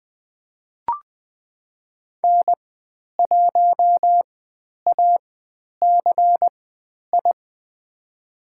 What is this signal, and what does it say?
Morse code, heard as a keyed tone, about 20 wpm, spaced as for 10 wpm, 700 Hz. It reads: N1ACI
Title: 102 Lesson 17g - Letter u - Call Signs 20wpm